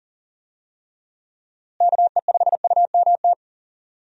Morse code, heard as a keyed tone, about 40 wpm, 700 Hz. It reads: KE5UMT
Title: USA Call Signs - 6 Characters - Repeated 2x - 40wpm